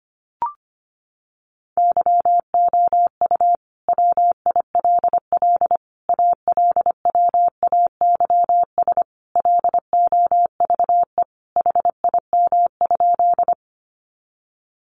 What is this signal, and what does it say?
Morse code, heard as a keyed tone, about 25 wpm, 700 Hz. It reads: YOU WILL ALWAYS LOVE HIM?
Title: Sentences from Top 400 Words 25wpm